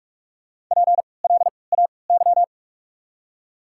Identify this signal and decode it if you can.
Morse code, heard as a keyed tone, about 48 wpm, spaced as for 26 wpm, 690 Hz. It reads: PLAY
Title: 3x Character Spacing - Top 300 Words - 45wpm